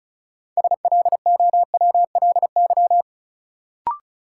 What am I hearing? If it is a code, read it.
Morse code, heard as a keyed tone, about 34 wpm, 700 Hz. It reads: SLOWLY
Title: Sentences from Top 1100 Words - New Word Drill - 35wpm